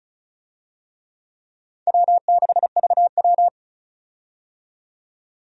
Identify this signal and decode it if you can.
Morse code, heard as a keyed tone, about 36 wpm, 700 Hz. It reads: W6VW
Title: USA Call Signs - 4 Characters - 35wpm